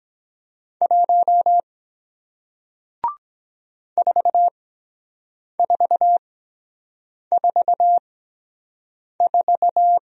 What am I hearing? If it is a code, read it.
Morse code, heard as a keyed tone, about 23 wpm, spaced as for 7 wpm, 700 Hz. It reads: 1 4444